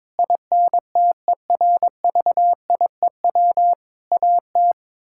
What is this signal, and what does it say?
Morse code, heard as a keyed tone, about 22 wpm, 700 Hz. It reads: INTERVIEW AT
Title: Sentences from Top 1600 Words 22wpm